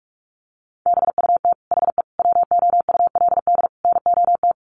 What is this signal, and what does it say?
Morse code, heard as a keyed tone, about 44 wpm, 700 Hz. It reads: BUT HE WOULD NOT
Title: Sentences from Top 900 Words 45wpm